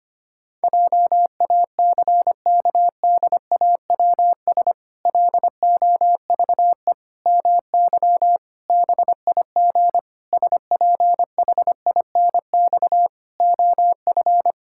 Morse code, heard as a keyed tone, about 25 wpm, 700 Hz. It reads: JACKDAWS LOVE MY BIG SPHINX OF
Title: Pangram Challenge v2 - 25wpm